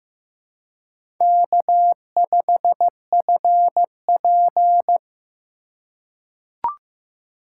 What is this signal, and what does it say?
Morse code, heard as a keyed tone, about 15 wpm, 700 Hz. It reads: K5FP
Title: USA Call Signs - 4x Character Spacing - 15wpm